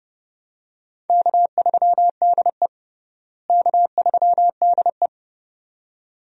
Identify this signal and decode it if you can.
Morse code, heard as a keyed tone, about 30 wpm, 700 Hz. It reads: K3DE K3DE
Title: USA Call Signs - 4 Characters - Repeated 2x - 30wpm